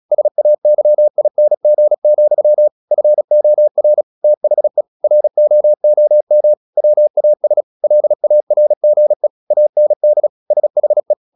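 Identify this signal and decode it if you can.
Morse code, heard as a keyed tone, about 36 wpm, 600 Hz. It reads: SAYING, FOR THE ROOM WAS LARGE AND SHE
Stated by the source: Morse practice at 36 wpm, text of Ken Follett's novel Never, from CW med Gnister.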